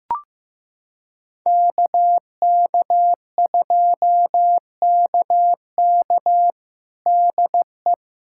Morse code, heard as a keyed tone, about 15 wpm, 700 Hz. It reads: KK2KK DE
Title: USA Callsign DE Callsign - 5 Characters - 15wpm